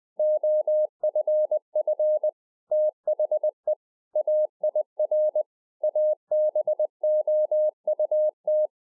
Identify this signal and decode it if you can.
Morse code, heard as a keyed tone, about 20 wpm, 610 Hz. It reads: OFF THE AIR ABOUT